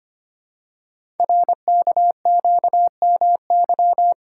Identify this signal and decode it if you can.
Morse code, heard as a keyed tone, about 25 wpm, 700 Hz. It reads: RKQMY
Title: Code Groups - 5 Letter:Numbers - 25wpm